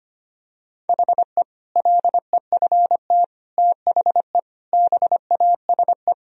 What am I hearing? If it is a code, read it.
Morse code, heard as a keyed tone, about 25 wpm, 700 Hz. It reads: HE LEFT THE BASE
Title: Sentences from Top 1000 Words - Review Part 1 - 25wpm